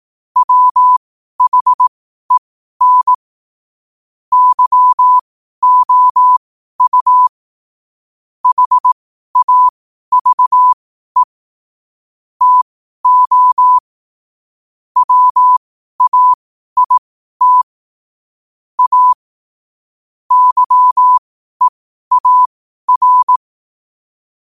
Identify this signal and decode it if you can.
Morse code, heard as a keyed tone, about 18 wpm, spaced as for 12 wpm, 1000 Hz. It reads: WHEN YOU HAVE TO WAIT A YEAR
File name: qotdfiles / 12wpm